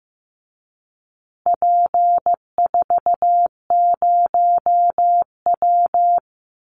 Morse code, heard as a keyed tone, about 15 wpm, 700 Hz. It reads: P40W
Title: Call Signs Repeated 3x - 15wpm